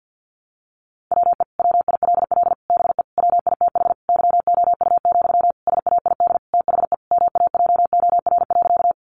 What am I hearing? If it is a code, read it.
Morse code, heard as a keyed tone, about 50 wpm, 700 Hz. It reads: WE WILL BE WITH YOU, SAID THE MAJOR.
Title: Sentences from Top 600 Words 50wpm